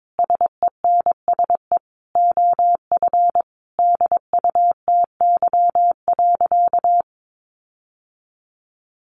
Morse code, heard as a keyed tone, about 22 wpm, 700 Hz. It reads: SENSE OF DUTY.
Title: Sentences from Top 1500 Words 22wpm